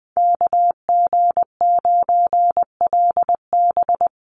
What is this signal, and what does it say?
Morse code, heard as a keyed tone, about 20 wpm, 700 Hz. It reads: KG9LB
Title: Call Signs Repeated 3x - 20wpm